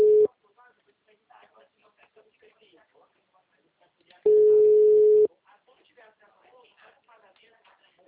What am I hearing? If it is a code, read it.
Ringback tone (Europe, 425 Hz), with speech.